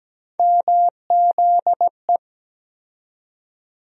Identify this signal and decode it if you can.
Morse code, heard as a keyed tone, about 17 wpm, 700 Hz. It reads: MZE
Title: USA Call Sign Postfixes - 3 Letters 17wpm